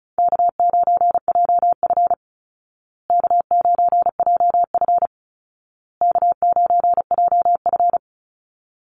Morse code, heard as a keyed tone, about 34 wpm, 700 Hz. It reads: K9JF K9JF K9JF
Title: Easy Call Signs Repeated 3x - 35wpm